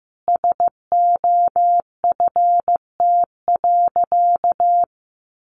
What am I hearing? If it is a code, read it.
Morse code, heard as a keyed tone, about 15 wpm, 700 Hz. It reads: SOFT.
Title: Sentences from Top 1500 Words 15wpm